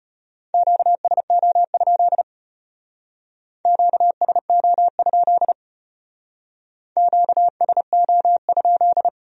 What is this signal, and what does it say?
Morse code, heard as a keyed tone, about 32 wpm, 700 Hz. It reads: QSO? QSO? QSO?